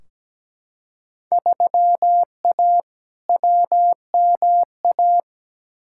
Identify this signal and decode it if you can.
Morse code, heard as a keyed tone, about 17 wpm, 700 Hz. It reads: 3A WMA
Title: Field Day Exchanges - 17wpm